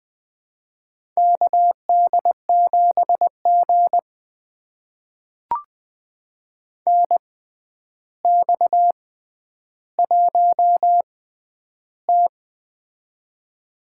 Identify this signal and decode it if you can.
Morse code, heard as a keyed tone, about 20 wpm, spaced as for 10 wpm, 700 Hz. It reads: KD7G NX1T
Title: USA Call Signs - 4 Characters - 6x Character Spacing - 20wpm